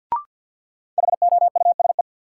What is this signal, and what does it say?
Morse code, heard as a keyed tone, about 53 wpm, 700 Hz. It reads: HOUSE
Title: Sentences from Top 200 Words - New Word Drill - 50wpm